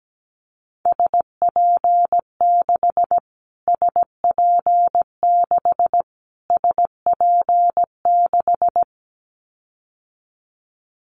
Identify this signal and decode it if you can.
Morse code, heard as a keyed tone, about 17 wpm, 700 Hz. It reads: SP6 SP6 SP6
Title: Callsign Prefixes for Top 25 DXCC - Repeated 3x - 17wpm